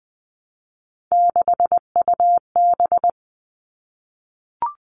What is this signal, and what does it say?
Morse code, heard as a keyed tone, about 20 wpm, 700 Hz. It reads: 6UB
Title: All Cut Numbers - 3-Digit 20wpm